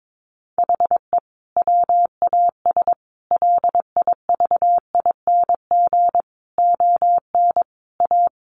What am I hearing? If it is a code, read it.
Morse code, heard as a keyed tone, about 22 wpm, 700 Hz. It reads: HE WAS LIVING ON A
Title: Sentences from Top 1200 Words 22wpm